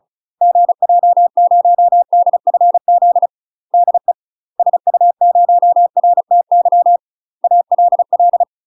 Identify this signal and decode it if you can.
Morse code, heard as a keyed tone, about 34 wpm, 700 Hz. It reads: GJ0DFZ DE SU0RTY ALL
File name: UK MASTER QSOs TRAINING / 35WPM / UK MASTER QSOs 0025